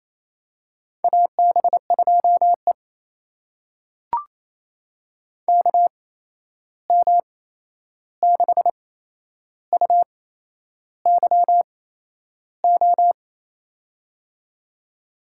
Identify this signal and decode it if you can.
Morse code, heard as a keyed tone, about 28 wpm, spaced as for 11 wpm, 700 Hz. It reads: AB2E KM6UYO
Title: USA Call Signs - 8x Character Spacing - 28wpm